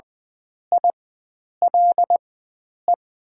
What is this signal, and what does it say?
Morse code, heard as a keyed tone, about 20 wpm, spaced as for 9 wpm, 700 Hz. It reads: ILE
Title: 4x Character Spacing - Top 500 Words - 20wpm